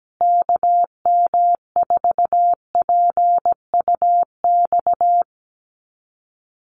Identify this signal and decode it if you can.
Morse code, heard as a keyed tone, about 17 wpm, 700 Hz. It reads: KM4PUX
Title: USA Call Signs Repeated 3x 17wpm